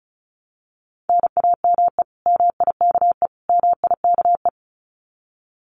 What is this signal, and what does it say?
Morse code, heard as a keyed tone, about 34 wpm, 700 Hz. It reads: NAME MIKE MIKE
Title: Common QSO Phrases Encore 35wpm